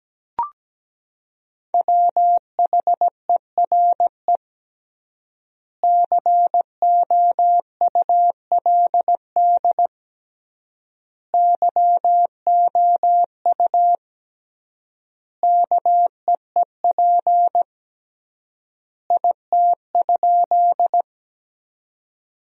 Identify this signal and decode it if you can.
Morse code, heard as a keyed tone, about 17 wpm, 700 Hz. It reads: WHERE COULD YOU KEEP IT?
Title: 3x Word-Spacing - Sentences from Top 300 Words - 17wpm